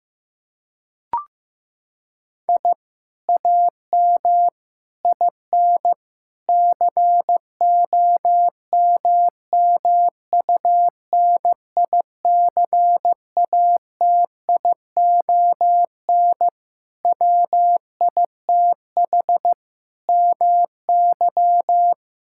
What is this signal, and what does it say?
Morse code, heard as a keyed tone, about 15 wpm, 700 Hz. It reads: I AM IN COMMUNICATION WITH MY
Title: Sentences from Top 1000 Words - Review Part 6 - 15wpm